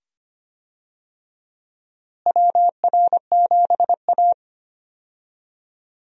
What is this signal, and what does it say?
Morse code, heard as a keyed tone, about 25 wpm, 700 Hz. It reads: WR7A
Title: USA Call Signs - 4 Characters - Repeated 2x - 25wpm